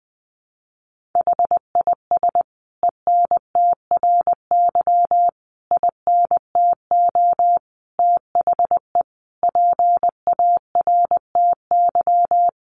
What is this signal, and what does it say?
Morse code, heard as a keyed tone, about 20 wpm, 700 Hz. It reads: HIS ENTRY INTO THE PARTY
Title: Sentences from Top 1500 Words 20wpm